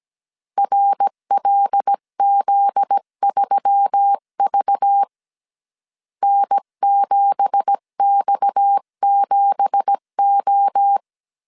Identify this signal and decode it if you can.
Morse code, heard as a keyed tone, about 17 wpm, 800 Hz. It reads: RLZ3V N7X7O
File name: CW / DET / 1 9 1500C S B 16wpm 2856